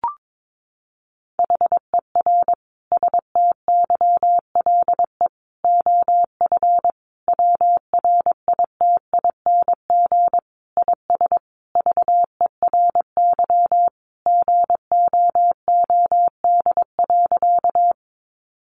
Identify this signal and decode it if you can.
Morse code, heard as a keyed tone, about 22 wpm, 700 Hz. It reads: HER STYLE OF WRITING IS VERY GOOD.
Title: Sentences from Top 800 Words 22wpm